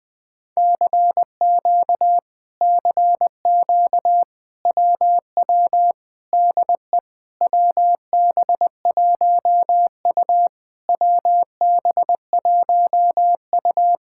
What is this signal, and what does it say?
Morse code, heard as a keyed tone, about 20 wpm, 700 Hz. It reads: CQ CQ WW DE WB1U WB1U
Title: Speed-Racing Common QSO Phrases - 17wpm